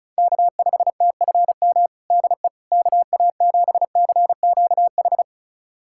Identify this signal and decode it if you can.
Morse code, heard as a keyed tone, about 36 wpm, 700 Hz. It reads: K5TFM DE KA7CQH
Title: USA Callsign DE Callsign 35wpm